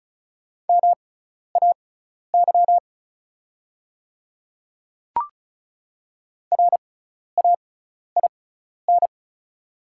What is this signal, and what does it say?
Morse code, heard as a keyed tone, about 36 wpm, spaced as for 12 wpm, 700 Hz. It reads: MAY RAIN
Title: 6x Character Spacing - Top 300 Words - 35wpm